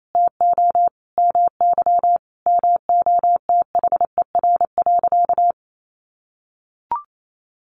Morse code, heard as a keyed tone, about 28 wpm, 700 Hz. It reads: TO MY MOTHER.
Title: Sentences from Top 1400 Words 28wpm